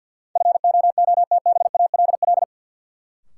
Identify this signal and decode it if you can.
Morse code, heard as a keyed tone, about 52 wpm, 700 Hz. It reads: FOOTBALL
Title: Sentences from Top 1600 Words - New Word Drill - 50wpm